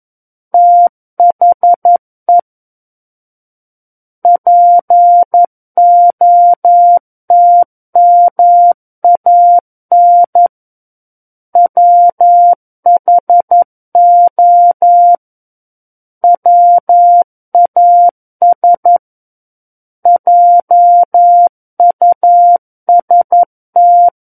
Morse code, heard as a keyed tone, about 11 wpm, 700 Hz. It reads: THE POTMAN WHO WAS JUST
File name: Book One / 02-Chapter 2 - 11 WPM